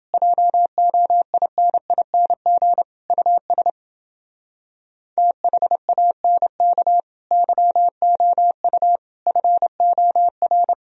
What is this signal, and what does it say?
Morse code, heard as a keyed tone, about 28 wpm, 700 Hz. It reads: JOINING US THANK YOU FOR